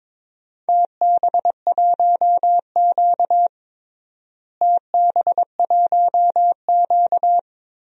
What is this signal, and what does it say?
Morse code, heard as a keyed tone, about 22 wpm, 700 Hz. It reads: TB1Q TB1Q
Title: DXCC-100 Call Signs - 4 Characters - Repeated 3x - 22wpm